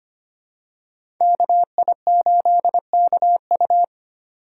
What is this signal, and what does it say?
Morse code, heard as a keyed tone, about 25 wpm, 700 Hz. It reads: KI8KU
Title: USA Call Signs - 5 Characters - 3x Character Spacing - 25wpm